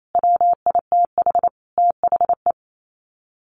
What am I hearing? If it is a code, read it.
Morse code, heard as a keyed tone, about 28 wpm, 700 Hz. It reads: WITH THE
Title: Sets of 5 Words Encore 28wpm